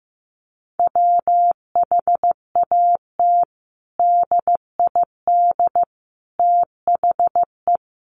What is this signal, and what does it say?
Morse code, heard as a keyed tone, about 15 wpm, 700 Hz. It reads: WHAT DID THE
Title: Sentences from Top 500 Words 15wpm